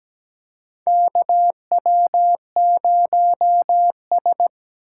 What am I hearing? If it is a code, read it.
Morse code, heard as a keyed tone, about 17 wpm, 700 Hz. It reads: KW0S
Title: USA Call Signs - 4 Characters - 3x Character Spacing - 17wpm